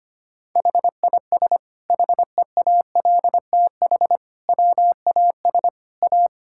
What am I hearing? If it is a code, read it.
Morse code, heard as a keyed tone, about 25 wpm, 700 Hz. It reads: HIS HEALTH WAS A
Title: Sentences from Top 1600 Words 25wpm